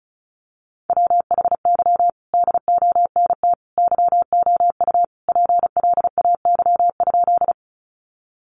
Morse code, heard as a keyed tone, about 34 wpm, 700 Hz. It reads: WHY DONT YOU PLAY?
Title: Sentences from Top 300 Words 35wpm